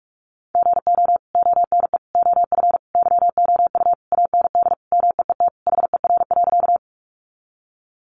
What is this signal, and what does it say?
Morse code, heard as a keyed tone, about 44 wpm, 700 Hz. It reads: GO ONE OF YOU AND MEET HER.
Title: Sentences from Top 1100 Words 45wpm